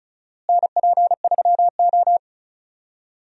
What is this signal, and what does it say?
Morse code, heard as a keyed tone, about 36 wpm, 700 Hz. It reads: NP3O
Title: USA Call Signs - 4 Characters - Repeated 2x - 35wpm